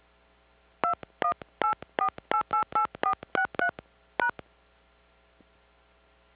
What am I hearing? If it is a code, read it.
DTMF, heard as telephone touch tones, tones of about 90 ms. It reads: 2184888463 #